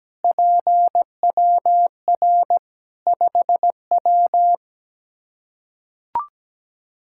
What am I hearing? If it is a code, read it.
Morse code, heard as a keyed tone, about 17 wpm, 700 Hz. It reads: PWR 5W
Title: Practice QSO - Exchange 2 - Info - 17wpm